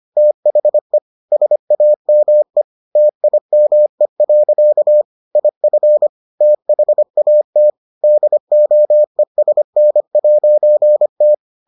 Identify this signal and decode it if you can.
Morse code, heard as a keyed tone, about 25 wpm, 600 Hz. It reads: THE SAME TIME. IF THAT DOESN'T